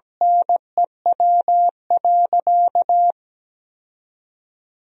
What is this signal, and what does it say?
Morse code, heard as a keyed tone, about 17 wpm, 700 Hz. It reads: NEW.
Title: 2x Word-Spacing - Sentences from Top 400 Words - 17wpm